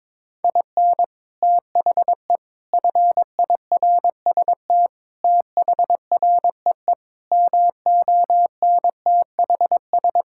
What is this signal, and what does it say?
Morse code, heard as a keyed tone, about 22 wpm, 700 Hz. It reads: IN THE FIRST THREE MONTHS
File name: practice / 22wpm / Sets of 8 Words - 22wpm